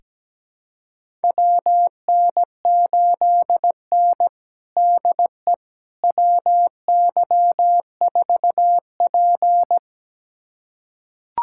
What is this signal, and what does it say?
Morse code, heard as a keyed tone, about 17 wpm, 700 Hz. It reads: WN8N DE WY4P
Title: USA Callsign DE Callsign - 4 Characters - 17wpm